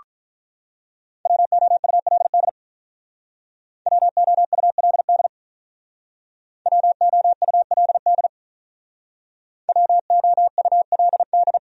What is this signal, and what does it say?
Morse code, heard as a keyed tone, about 46 wpm, 700 Hz. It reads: WOULD WOULD WOULD WOULD